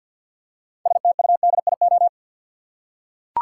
Morse code, heard as a keyed tone, about 52 wpm, 700 Hz. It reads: STUDIO
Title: Six Letter Words - 50wpm